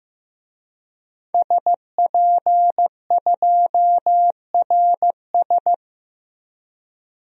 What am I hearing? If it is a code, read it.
Morse code, heard as a keyed tone, about 15 wpm, 700 Hz. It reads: SP2RS
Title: DXCC-100 Call Signs - 5 Characters - 15wpm